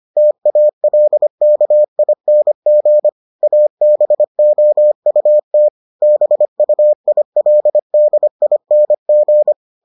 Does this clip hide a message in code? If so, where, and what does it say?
Yes: Morse code from 0.2 to 9.5 s, heard as a keyed tone, about 25 wpm, 600 Hz. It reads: TALKING ABOUT BUILDING